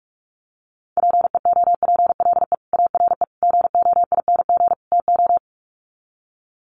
Morse code, heard as a keyed tone, about 44 wpm, 700 Hz. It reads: PEOPLE ARE GOING TO
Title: Sets of 4 Words 45wpm